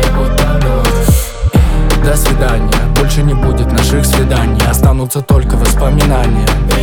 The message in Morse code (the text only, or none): none